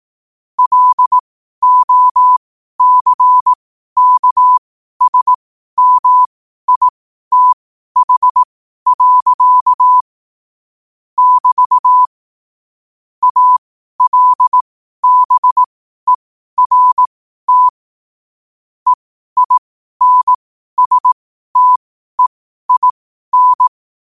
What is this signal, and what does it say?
Morse code, heard as a keyed tone, about 18 wpm, spaced as for 12 wpm, 1000 Hz. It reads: LOCKSMITH. <BT> ALBERT EINSTEIN
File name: qotdfiles / 12wpm